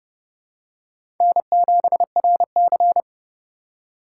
Morse code, heard as a keyed tone, about 30 wpm, 700 Hz. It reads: N7RC
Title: USA Call Signs - 4 Characters - 4x Character Spacing - 30wpm